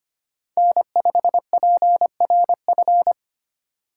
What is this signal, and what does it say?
Morse code, heard as a keyed tone, about 30 wpm, 700 Hz. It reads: N5PRF
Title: Speed Racing - USA 1x3 Call Signs 25wpm